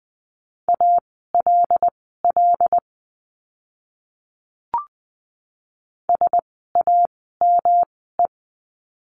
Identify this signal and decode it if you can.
Morse code, heard as a keyed tone, about 20 wpm, spaced as for 14 wpm, 700 Hz. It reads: ALL SAME
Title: 237 Lesson 38f - Letter q - Words 20wpm